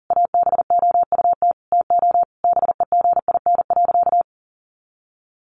Morse code, heard as a keyed tone, about 40 wpm, 700 Hz. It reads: ABOUT TO BEGIN.
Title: Sentences from Top 1200 Words 40wpm